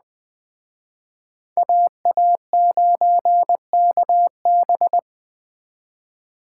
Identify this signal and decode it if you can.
Morse code, heard as a keyed tone, about 21 wpm, 700 Hz. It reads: AA9KB
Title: Speed Racing - USA Call Signs 17wpm